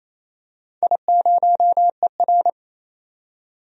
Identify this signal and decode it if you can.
Morse code, heard as a keyed tone, about 28 wpm, 700 Hz. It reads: I0ER
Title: DXCC-100 Call Signs - 4 Characters - 28wpm